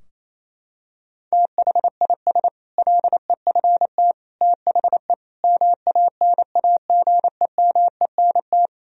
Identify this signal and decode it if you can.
Morse code, heard as a keyed tone, about 28 wpm, 700 Hz. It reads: THIS LEFT THE MANAGEMENT
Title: Sentences from Top 1000 Words - Review Part 4 - 28wpm